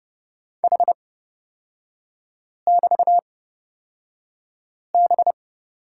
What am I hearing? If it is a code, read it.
Morse code, heard as a keyed tone, about 31 wpm, spaced as for 5 wpm, 700 Hz. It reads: H<BT>B
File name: Lesson 38e - Letter q - Three Characters 30wpm